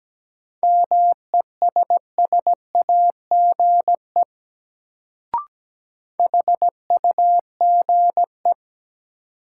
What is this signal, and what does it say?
Morse code, heard as a keyed tone, about 17 wpm, 700 Hz. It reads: MESSAGE HUGE
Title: Sentences from Top 1100 Words - New Word Drill - 17wpm